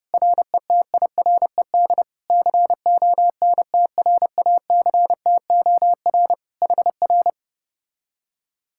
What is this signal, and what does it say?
Morse code, heard as a keyed tone, about 29 wpm, 700 Hz. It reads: RETIRED CONTRACTOR HR